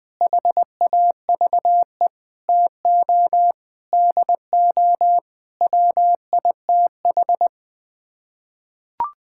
Morse code, heard as a keyed tone, about 20 wpm, 700 Hz. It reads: HAVE TO DO WITH
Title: Sets of 8 Words - 20wpm